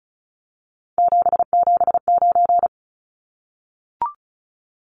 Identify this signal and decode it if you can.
Morse code, heard as a keyed tone, about 34 wpm, 700 Hz. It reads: 779
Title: Numbers - 3-Digit 35wpm